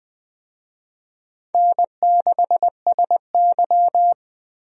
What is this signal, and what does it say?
Morse code, heard as a keyed tone, about 21 wpm, 700 Hz. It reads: N6SY
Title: USA Call Signs - 4 Characters - Repeated 3x - 20wpm